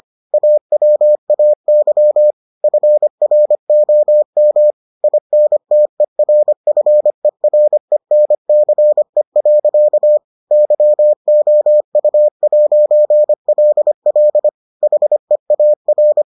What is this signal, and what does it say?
Morse code, heard as a keyed tone, about 25 wpm, 600 Hz. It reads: AWAY FROM INTERFERENCE. YOU'LL HEAR